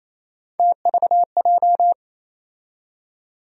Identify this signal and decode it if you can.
Morse code, heard as a keyed tone, about 27 wpm, 700 Hz. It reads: TVJ
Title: USA Call Sign Postfixes - 3 Letters 28wpm